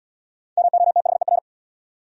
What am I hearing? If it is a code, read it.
Morse code, heard as a keyed tone, about 69 wpm, 700 Hz. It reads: MOTHER